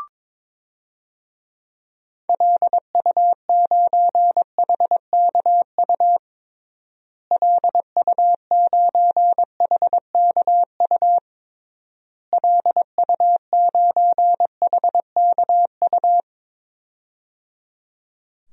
Morse code, heard as a keyed tone, about 22 wpm, 700 Hz. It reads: LU9HKU LU9HKU LU9HKU
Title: DXCC-100 Call Signs - 6 Characters - Repeated 3x - 22wpm